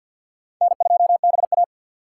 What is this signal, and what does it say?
Morse code, heard as a keyed tone, about 47 wpm, 700 Hz. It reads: N2BA
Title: USA Call Signs - 3x Character Spacing - 50wpm